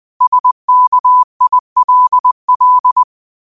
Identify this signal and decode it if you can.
Morse code, heard as a keyed tone, about 20 wpm, 1000 Hz. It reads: SKILL
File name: qotdfiles / 20wpm